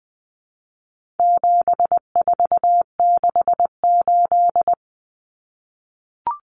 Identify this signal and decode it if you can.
Morse code, heard as a keyed tone, about 20 wpm, 700 Hz. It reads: 7468
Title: Numbers - 4-Digit 20wpm